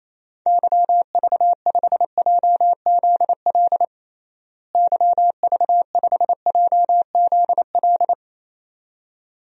Morse code, heard as a keyed tone, about 28 wpm, 700 Hz. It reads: YV5JZL YV5JZL
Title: DXCC-100 Call Signs - 6 Characters - Repeated 2x - 28wpm